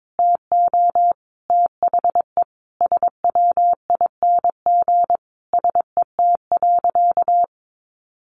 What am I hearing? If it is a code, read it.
Morse code, heard as a keyed tone, about 22 wpm, 700 Hz. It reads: TO THE SWING SET.
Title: Additional Easy Phrases 22wpm